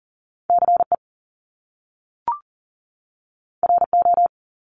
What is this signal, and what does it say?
Morse code, heard as a keyed tone, about 40 wpm, 700 Hz. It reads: CE RO
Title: Top Two Letter Combinations in Words 40wpm